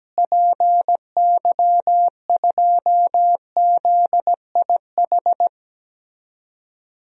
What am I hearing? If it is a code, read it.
Morse code, heard as a keyed tone, about 17 wpm, 700 Hz. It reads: PY2ZIH
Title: DXCC-100 Call Signs - 6 Characters - Repeated 3x - 17wpm